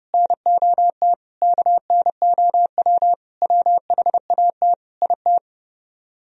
Morse code, heard as a keyed tone, about 37 wpm, 700 Hz. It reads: NOT KNOW WHAT IT
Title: Speed-Racing - Sets of 5 Words 30wpm